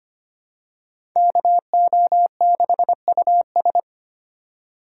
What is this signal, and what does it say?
Morse code, heard as a keyed tone, about 25 wpm, 700 Hz. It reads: KO6US
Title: USA Call Signs - 5 Characters - 2x Character Spacing - 25wpm